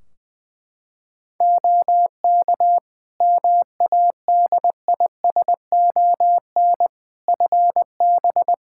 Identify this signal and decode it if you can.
Morse code, heard as a keyed tone, about 21 wpm, 700 Hz. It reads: OK MADISON FB
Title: Practice QSO - Exchange 2 - Info - 20wpm